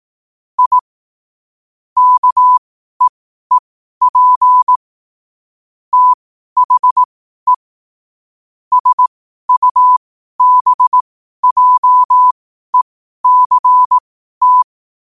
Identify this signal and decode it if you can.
Morse code, heard as a keyed tone, about 18 wpm, spaced as for 12 wpm, 1000 Hz. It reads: I KEEP THE SUBJECT
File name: qotdfiles / 12wpm